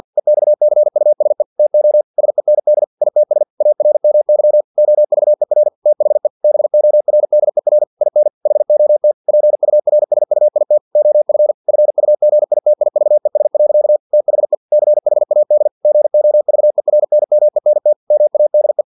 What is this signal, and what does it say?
Morse code, heard as a keyed tone, about 50 wpm, 600 Hz. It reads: EXCUSE TO SEND ITS ARMY OVER THE BORDER IN HOT PURSUIT OF FUGITIVES. THE CHAD GOVERNMENT MADE
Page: Never 49wpm